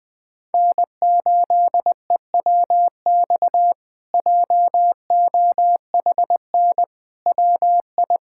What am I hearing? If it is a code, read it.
Morse code, heard as a keyed tone, about 20 wpm, 700 Hz. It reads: N8EWX JOHN WI
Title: K1USN SST QSO Practice - 20wpm